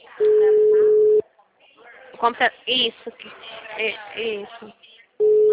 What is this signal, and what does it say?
Ringback tone (Europe, 425 Hz), with speech.
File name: out- -2041- - - .7914